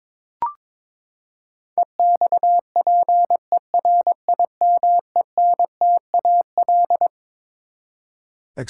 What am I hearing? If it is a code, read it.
Morse code, heard as a keyed tone, about 22 wpm, 700 Hz. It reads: EXPERIMENTAL
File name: Twelve Letter Words - 22wpm